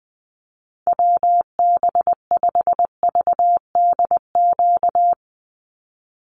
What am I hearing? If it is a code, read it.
Morse code, heard as a keyed tone, about 20 wpm, 700 Hz. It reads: WB5VDQ
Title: USA Call Signs 20wpm